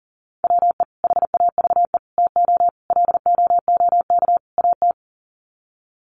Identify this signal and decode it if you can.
Morse code, heard as a keyed tone, about 40 wpm, 700 Hz. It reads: WE HAVE TO LOOK AT
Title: Sets of 5 Words Encore2 40wpm